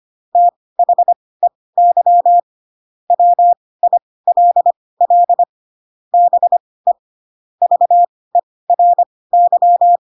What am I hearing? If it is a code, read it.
Morse code, heard as a keyed tone, about 25 wpm, spaced as for 18 wpm, 700 Hz. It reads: THEY WILL BE VERY
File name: Pinocchio / Pinocchio ch4 12 25 ik1hge